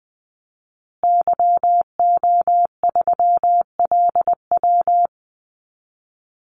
Morse code, heard as a keyed tone, about 20 wpm, 700 Hz. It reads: YO3LW
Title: Call Signs Repeated 2x - 20wpm